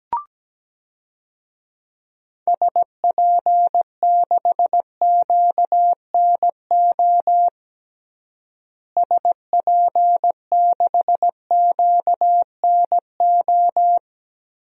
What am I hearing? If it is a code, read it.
Morse code, heard as a keyed tone, about 17 wpm, 700 Hz. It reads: SP6QNO SP6QNO
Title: DXCC-100 Call Signs - 6 Characters - Repeated 3x - 17wpm